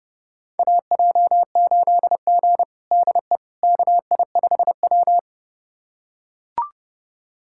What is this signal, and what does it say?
Morse code, heard as a keyed tone, about 32 wpm, 700 Hz. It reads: AJ8G DE KI5W
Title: USA Callsign DE Callsign - 4 Characters - 30wpm